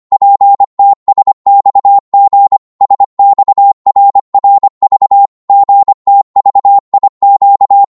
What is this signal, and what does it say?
Morse code, heard as a keyed tone, about 25 wpm, 800 Hz. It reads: PTSXG SXRRV GTVIQ
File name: Morse / ltr-25wpm